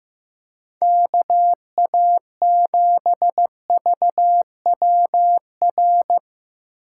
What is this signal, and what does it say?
Morse code, heard as a keyed tone, about 15 wpm, 700 Hz. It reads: KA7VWR
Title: USA Call Signs - 3x Character Spacing - 15wpm